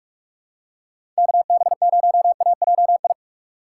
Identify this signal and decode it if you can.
Morse code, heard as a keyed tone, about 41 wpm, 700 Hz. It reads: KB0AJI
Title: USA Call Signs - 4x Character Spacing - 45wpm